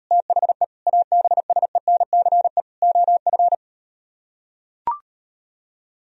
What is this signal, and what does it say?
Morse code, heard as a keyed tone, about 38 wpm, 700 Hz. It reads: THE ABSENCE OF